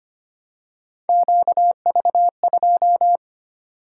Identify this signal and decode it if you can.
Morse code, heard as a keyed tone, about 25 wpm, 700 Hz. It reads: QV2
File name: Three Letter-Number Combinations - 25wpm